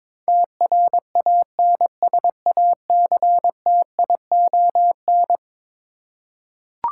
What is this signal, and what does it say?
Morse code, heard as a keyed tone, about 22 wpm, 700 Hz. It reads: TRANSACTION
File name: practice / 22wpm / Eleven Letter Words - 22wpm